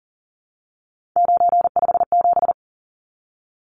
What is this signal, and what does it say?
Morse code, heard as a keyed tone, about 40 wpm, 700 Hz. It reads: 957
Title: Numbers - 3-Digit 40wpm